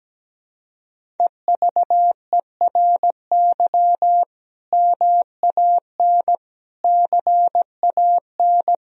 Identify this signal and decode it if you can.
Morse code, heard as a keyed tone, about 17 wpm, 700 Hz. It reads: EVERY MAN CAN
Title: Sentences from Top 1000 Words - Review Part 1 - 17wpm